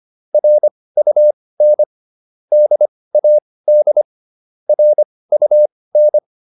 Morse code, heard as a keyed tone, about 25 wpm, spaced as for 18 wpm, 600 Hz. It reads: RUN DAD RUN